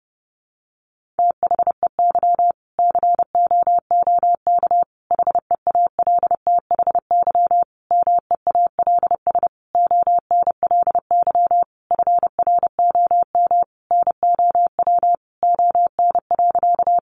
Morse code, heard as a keyed tone, about 30 wpm, 700 Hz. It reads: THEY COOK HEALTHY MEALS ONLY FROM NOW ON.
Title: Easy Phrases 30wpm